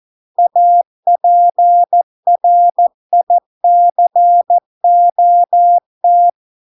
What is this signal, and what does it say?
Morse code, heard as a keyed tone, about 14 wpm, 700 Hz. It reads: APRICOT